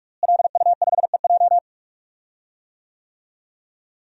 Morse code, heard as a keyed tone, about 46 wpm, 700 Hz. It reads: LU5EJ
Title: DXCC-100 Call Signs - 5 Characters - Repeated 2x - 45wpm